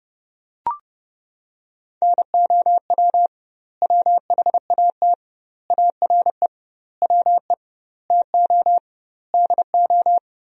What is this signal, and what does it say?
Morse code, heard as a keyed tone, about 30 wpm, 700 Hz. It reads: NOW WHAT ARE WE TO DO
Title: 2x Word-Spacing - Sentences from Top 100 Words - 30wpm